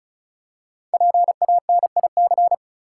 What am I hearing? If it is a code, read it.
Morse code, heard as a keyed tone, about 34 wpm, 700 Hz. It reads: PANIC
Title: Five Letter Words - 35wpm